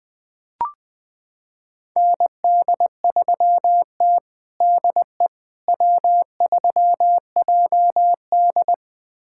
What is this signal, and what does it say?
Morse code, heard as a keyed tone, about 21 wpm, 700 Hz. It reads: ND3T DE W3JD
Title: USA Callsign DE Callsign - 4 Characters - 20wpm